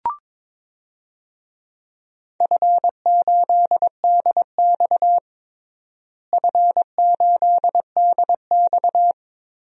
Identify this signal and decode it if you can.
Morse code, heard as a keyed tone, about 22 wpm, 700 Hz. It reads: F8DX F8DX
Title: DXCC-100 Call Signs - 4 Characters - Repeated 2x - 22wpm